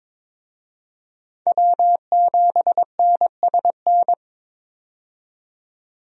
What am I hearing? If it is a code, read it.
Morse code, heard as a keyed tone, about 21 wpm, 700 Hz. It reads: W7NSN
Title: USA Call Signs - 5 Characters - 22wpm